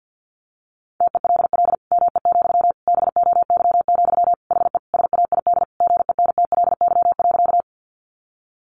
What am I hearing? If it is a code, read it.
Morse code, heard as a keyed tone, about 50 wpm, 700 Hz. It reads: TELL ME, BOY, HE SAID GENTLY.
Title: Sentences from Top 1500 Words 50wpm